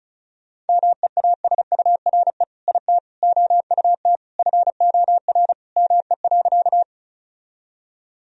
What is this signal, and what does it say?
Morse code, heard as a keyed tone, about 36 wpm, 700 Hz. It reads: MEASURE IT OUT FOR ME.
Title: Sentences from Top 1000 Words - Review Part 2 - 35wpm